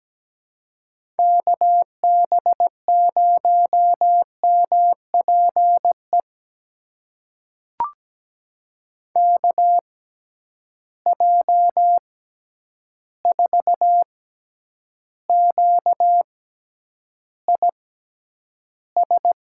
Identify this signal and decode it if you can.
Morse code, heard as a keyed tone, about 17 wpm, spaced as for 9 wpm, 700 Hz. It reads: KB0MPE KJ4QIS
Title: USA Call Signs - 6 Characters - 6x Character Spacing - 17wpm